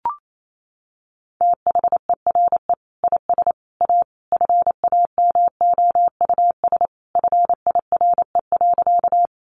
Morse code, heard as a keyed tone, about 28 wpm, 700 Hz. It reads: THERE IS A FAMOUS FIRE.
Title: Sentences from Top 1500 Words 28wpm